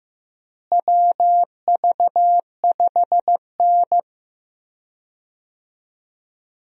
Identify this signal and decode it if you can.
Morse code, heard as a keyed tone, about 15 wpm, 700 Hz. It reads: WV5N
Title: USA Call Signs - 4 Characters - Repeated 3x - 15wpm